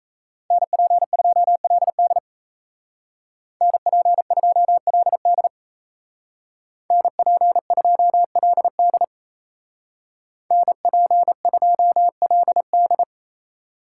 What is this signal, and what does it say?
Morse code, heard as a keyed tone, about 34 wpm, 700 Hz. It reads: NP2LD NP2LD NP2LD NP2LD
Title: Speed Racing - USA Call Signs 28wpm